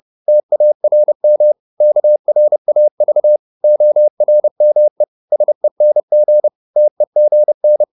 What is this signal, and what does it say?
Morse code, heard as a keyed tone, about 30 wpm, 600 Hz. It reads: TARM KRAV ORME SENG TEGN
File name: Korte ord DK 30wpm